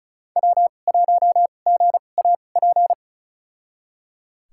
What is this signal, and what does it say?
Morse code, heard as a keyed tone, about 36 wpm, spaced as for 25 wpm, 700 Hz. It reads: W1GAP
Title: USA Call Signs - 2x Character Spacing - 35wpm